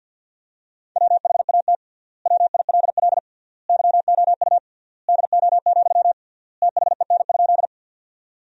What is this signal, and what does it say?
Morse code, heard as a keyed tone, about 50 wpm, 700 Hz. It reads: WHAT WILL YOU DO, THEN?
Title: 3x Word-Spacing - Sentences from Top 100 Words - 50wpm